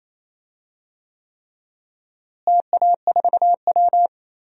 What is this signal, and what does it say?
Morse code, heard as a keyed tone, about 27 wpm, 700 Hz. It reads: TA4W
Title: DXCC-100 Call Signs - 4 Characters - Repeated 3x - 28wpm